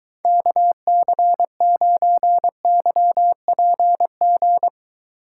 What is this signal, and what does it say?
Morse code, heard as a keyed tone, about 23 wpm, 700 Hz. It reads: KC9YPG